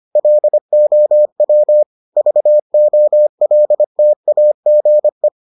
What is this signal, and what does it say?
Morse code, heard as a keyed tone, about 25 wpm, 600 Hz. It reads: LOW VOLTAGE